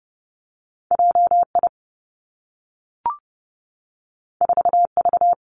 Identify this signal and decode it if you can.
Morse code, heard as a keyed tone, about 30 wpm, 700 Hz. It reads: J I 4 V
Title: Two Challenging Letter-Number Combinations 30wpm